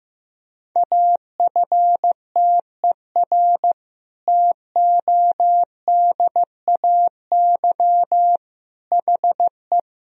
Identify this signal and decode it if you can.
Morse code, heard as a keyed tone, about 15 wpm, 700 Hz. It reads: AFTER TODAY HE
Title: Sentences from Top 1000 Words - Review Part 4 - 15wpm